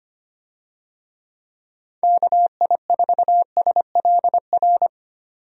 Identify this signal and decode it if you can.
Morse code, heard as a keyed tone, about 25 wpm, 700 Hz. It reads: KI4SLR
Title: USA Call Signs - 6 Characters - 25wpm